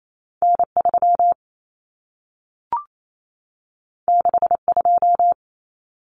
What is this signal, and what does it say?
Morse code, heard as a keyed tone, about 28 wpm, 700 Hz. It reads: N3 62
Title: Common Cut Numbers - 2-Digit 28wpm